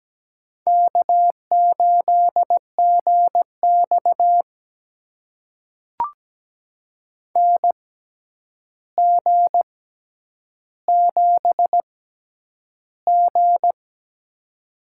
Morse code, heard as a keyed tone, about 17 wpm, spaced as for 9 wpm, 700 Hz. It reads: K8GX NG7G
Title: USA Call Signs - 4 Characters - 6x Character Spacing - 17wpm